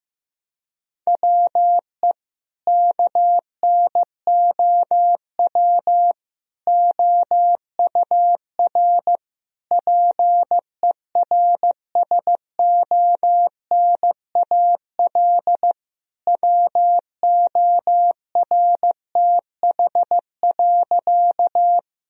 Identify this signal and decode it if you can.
Morse code, heard as a keyed tone, about 15 wpm, 700 Hz. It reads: WE KNOW OUR PERSONAL WORTH.
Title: Sentences from Top 1000 Words - Review Part 4 - 15wpm